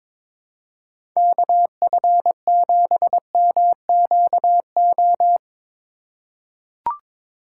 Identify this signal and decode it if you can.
Morse code, heard as a keyed tone, about 21 wpm, 700 Hz. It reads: KF7MQO
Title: USA Call Signs - 6 Characters - 2x Character Spacing - 22wpm